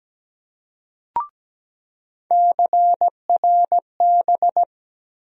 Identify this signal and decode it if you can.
Morse code, heard as a keyed tone, about 17 wpm, 700 Hz. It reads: CRB
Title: USA Call Sign Postfixes - 3 Letters 17wpm